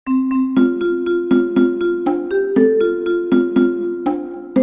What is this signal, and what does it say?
There is no code.